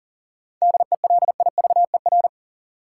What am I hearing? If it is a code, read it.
Morse code, heard as a keyed tone, about 40 wpm, 700 Hz. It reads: DELIVER
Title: Seven Letter Words - 40wpm